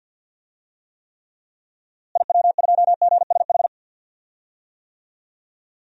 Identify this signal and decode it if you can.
Morse code, heard as a keyed tone, about 52 wpm, 700 Hz. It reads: IW2GSH
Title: DXCC-100 Call Signs - 6 Characters - Repeated 3x - 50wpm